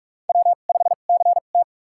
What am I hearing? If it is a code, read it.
Morse code, heard as a keyed tone, about 43 wpm, spaced as for 32 wpm, 700 Hz. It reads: W5CT
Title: USA Call Signs - 4 Characters - 2x Character Spacing - 45wpm